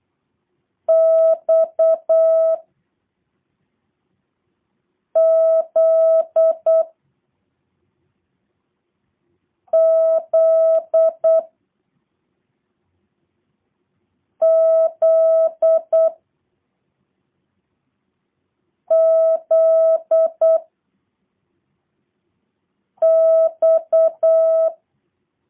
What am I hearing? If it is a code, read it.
Morse code, heard as a keyed tone, about 8 wpm, 650 Hz. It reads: X Z Z Z Z X